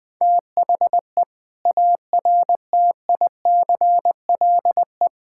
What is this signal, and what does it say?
Morse code, heard as a keyed tone, about 20 wpm, 700 Hz. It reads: THE ARTICLE
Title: Sentences from Top 1000 Words - Review Part 5 - 20wpm